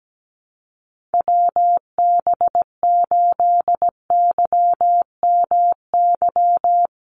Morse code, heard as a keyed tone, about 17 wpm, 700 Hz. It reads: WB8YMY
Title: USA Call Signs 17wpm